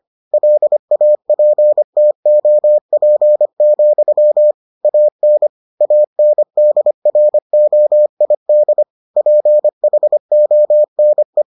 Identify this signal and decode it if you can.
Morse code, heard as a keyed tone, about 25 wpm, 600 Hz. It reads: LAPTOP, AN ANDROID PHONE